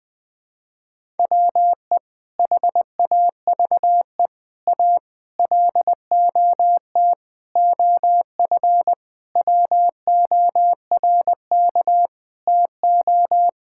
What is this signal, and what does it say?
Morse code, heard as a keyed tone, about 20 wpm, 700 Hz. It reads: WE HAVE A LOT OF WORK TO
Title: Sentences with 5-Word Phrases - 20wpm